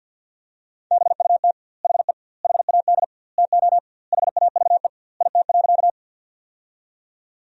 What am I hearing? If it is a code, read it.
Morse code, heard as a keyed tone, about 52 wpm, 700 Hz. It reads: BUT HE HAD TO HAVE IT.
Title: 2x Word-Spacing - Sentences from Top 100 Words - 50wpm